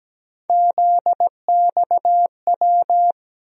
Morse code, heard as a keyed tone, about 17 wpm, 700 Hz. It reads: ZXW
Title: USA Call Sign Postfixes - 3 Letters 17wpm